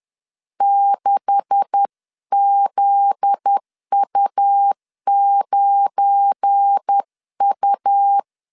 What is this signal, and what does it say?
Morse code, heard as a keyed tone, about 10 wpm, 800 Hz. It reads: 6ZU9U
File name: LEZIONI / 10 SIGNS / 10C DOT 10wpm 2940